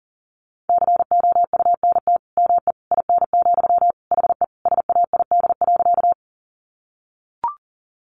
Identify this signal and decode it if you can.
Morse code, heard as a keyed tone, about 40 wpm, 700 Hz. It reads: COUNT ME IN, HE SAID.
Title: Sentences from Top 1500 Words 40wpm